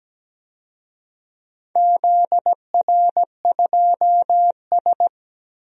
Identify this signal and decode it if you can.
Morse code, heard as a keyed tone, about 17 wpm, 700 Hz. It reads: ZR2S
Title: DXCC-100 Call Signs - 4 Characters - 17wpm